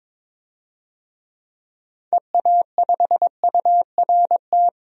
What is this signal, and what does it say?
Morse code, heard as a keyed tone, about 22 wpm, 700 Hz. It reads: EA5URT
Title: DXCC-100 Call Signs - 6 Characters - 22wpm